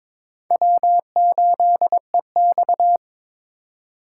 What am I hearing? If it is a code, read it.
Morse code, heard as a keyed tone, about 22 wpm, 700 Hz. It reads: W8EX
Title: USA Call Signs - 4 Characters - 4x Character Spacing - 22wpm